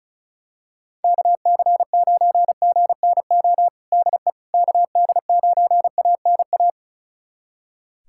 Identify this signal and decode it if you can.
Morse code, heard as a keyed tone, about 34 wpm, 700 Hz. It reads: KC9GNO DE KD9ANA
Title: USA Callsign DE Callsign - 6 Characters - 35wpm